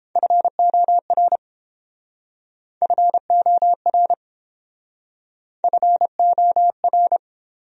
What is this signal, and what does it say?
Morse code, heard as a keyed tone, about 30 wpm, 700 Hz. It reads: FOR FOR FOR